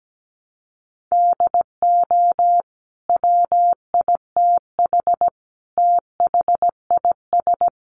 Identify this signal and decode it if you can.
Morse code, heard as a keyed tone, about 17 wpm, 700 Hz. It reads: DO WITH THIS
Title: Sentences from Top 1400 Words 17wpm